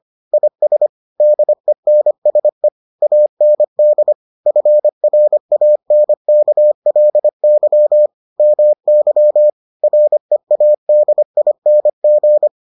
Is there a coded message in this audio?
Yes: Morse code from 0.3 to 12.5 s, heard as a keyed tone, about 25 wpm, 600 Hz. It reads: IS DENSE AND FRANKLY MY READING